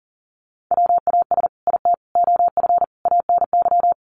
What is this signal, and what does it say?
Morse code, heard as a keyed tone, about 40 wpm, 700 Hz. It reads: WAS IT OF ANY
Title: Sentences from Top 1100 Words 40wpm